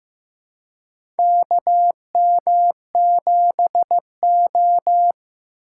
Morse code, heard as a keyed tone, about 15 wpm, 700 Hz. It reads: KM7O
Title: USA Call Signs - 4x Character Spacing - 15wpm